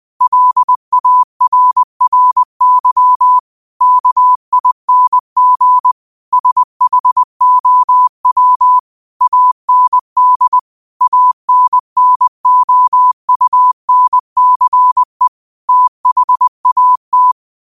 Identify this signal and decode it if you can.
Morse code, heard as a keyed tone, about 20 wpm, 1000 Hz. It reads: LARRY KING SHOW AND ANNOUNCE THAT
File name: qotdfiles / 20wpm